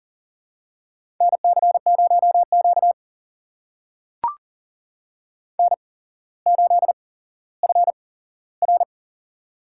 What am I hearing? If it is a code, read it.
Morse code, heard as a keyed tone, about 40 wpm, spaced as for 17 wpm, 700 Hz. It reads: NC0Q N8FR
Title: USA Call Signs - 4 Characters - 8x Character Spacing - 40wpm